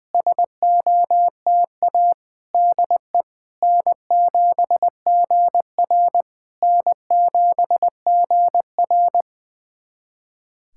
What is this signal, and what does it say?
Morse code, heard as a keyed tone, about 20 wpm, 700 Hz. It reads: SOTA DE N7GR N7GR
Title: SOTA QSO Practice 20wpm